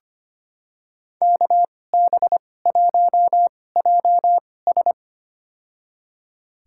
Morse code, heard as a keyed tone, about 25 wpm, spaced as for 18 wpm, 700 Hz. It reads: KB1JS
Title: USA Call Signs - 2x Character Spacing - 25wpm